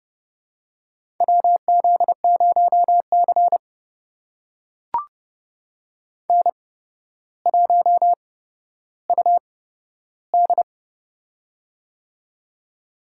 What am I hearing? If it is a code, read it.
Morse code, heard as a keyed tone, about 30 wpm, spaced as for 13 wpm, 700 Hz. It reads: WZ0C N1UD
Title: USA Call Signs - 8x Character Spacing - 30wpm